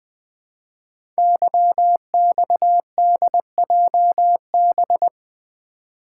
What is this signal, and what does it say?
Morse code, heard as a keyed tone, about 20 wpm, 700 Hz. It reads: YXDJB
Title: Code Groups - 5 Letters - 20wpm